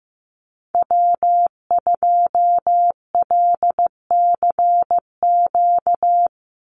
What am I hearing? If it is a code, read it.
Morse code, heard as a keyed tone, about 15 wpm, 700 Hz. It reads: W2LCQ
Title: Call Signs Repeated 3x - 15wpm